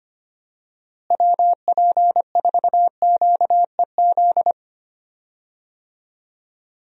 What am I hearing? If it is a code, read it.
Morse code, heard as a keyed tone, about 25 wpm, 700 Hz. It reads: WP4QEZ
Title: USA Call Signs - 6 Characters - Repeated 2x - 25wpm